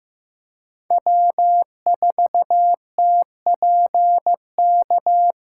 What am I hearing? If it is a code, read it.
Morse code, heard as a keyed tone, about 15 wpm, 700 Hz. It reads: W4TPK
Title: USA Callsign DE Callsign - 5 Characters - 15wpm